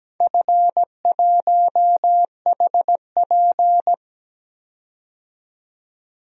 Morse code, heard as a keyed tone, about 17 wpm, 700 Hz. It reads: F1HP
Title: DXCC-100 Call Signs - 4 Characters - 17wpm